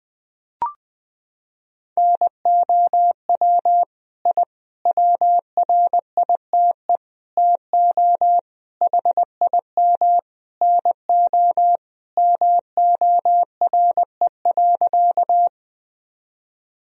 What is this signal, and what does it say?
Morse code, heard as a keyed tone, about 20 wpm, 700 Hz. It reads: NOW I WRITE TO HIM NO MORE.
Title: Sentences from Top 1000 Words - Review Part 1 - 20wpm